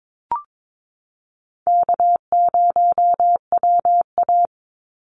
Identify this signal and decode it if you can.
Morse code, heard as a keyed tone, about 22 wpm, 700 Hz. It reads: K0WA
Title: Call Signs Repeated 3x - 22wpm